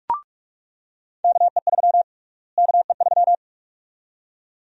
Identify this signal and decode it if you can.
Morse code, heard as a keyed tone, about 42 wpm, 700 Hz. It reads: KE3 KE3
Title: USA Call Sign Prefixes - Repeated 2x - 45wpm